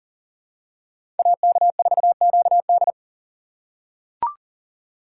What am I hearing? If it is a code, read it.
Morse code, heard as a keyed tone, about 40 wpm, 700 Hz. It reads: AK4QD
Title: USA Call Signs - 6x Character Spacing - 40wpm